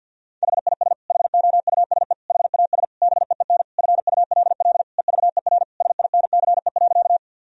Morse code, heard as a keyed tone, about 51 wpm, 700 Hz. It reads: HIS HOUSE HAS BEEN FULL EVER SINCE.
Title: Sentences from Top 1000 Words - Review Part 3 - 50wpm